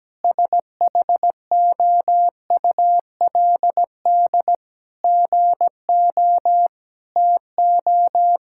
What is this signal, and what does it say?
Morse code, heard as a keyed tone, about 17 wpm, 700 Hz. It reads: SHOULD GO TO
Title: Sentences from Top 1600 Words 17wpm